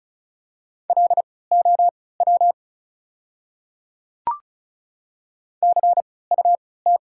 Morse code, heard as a keyed tone, about 36 wpm, spaced as for 20 wpm, 700 Hz. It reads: LOW CUT
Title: 3x Character Spacing - Top 300 Words - 35wpm